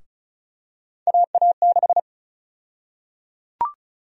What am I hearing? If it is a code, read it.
Morse code, heard as a keyed tone, about 34 wpm, 700 Hz. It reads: AA6
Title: USA Call Sign Prefixes - 35wpm